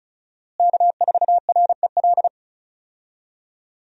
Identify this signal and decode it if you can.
Morse code, heard as a keyed tone, about 36 wpm, 700 Hz. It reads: K4REL
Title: USA Call Signs - 5 Characters - Repeated 3x - 35wpm